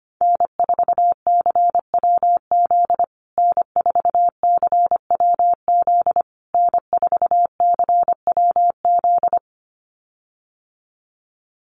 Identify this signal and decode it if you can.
Morse code, heard as a keyed tone, about 25 wpm, 700 Hz. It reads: N4CWZ N4CWZ N4CWZ
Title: USA Call Signs Repeated 3x 25wpm